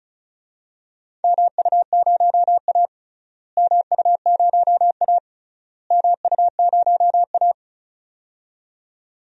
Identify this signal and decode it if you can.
Morse code, heard as a keyed tone, about 36 wpm, 700 Hz. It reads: MU0A MU0A MU0A
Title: DXCC-100 Call Signs - 4 Characters - Repeated 3x - 35wpm